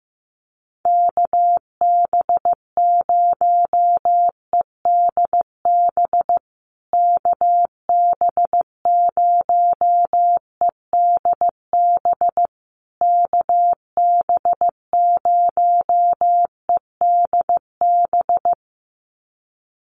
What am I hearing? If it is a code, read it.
Morse code, heard as a keyed tone, about 15 wpm, 700 Hz. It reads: KB0EDB KB0EDB KB0EDB
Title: USA Call Signs Repeated 3x 15wpm